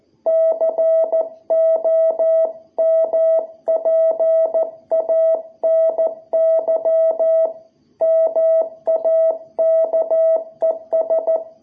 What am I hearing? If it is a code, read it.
Morse code, heard as a keyed tone, about 14 wpm, 660 Hz. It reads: COMPANY MAKES
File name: jh8cbh / 2021cw